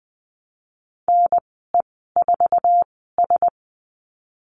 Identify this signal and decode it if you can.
Morse code, heard as a keyed tone, about 20 wpm, spaced as for 15 wpm, 700 Hz. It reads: NE4S
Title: 088 Lesson 15g - Number 5 - Call Signs 20wpm